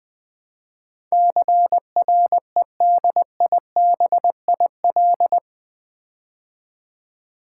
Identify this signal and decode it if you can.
Morse code, heard as a keyed tone, about 20 wpm, 700 Hz. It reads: CREDIBIL
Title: Build A Word - 4 Letters At A Time - 11-Letter Words - 20wpm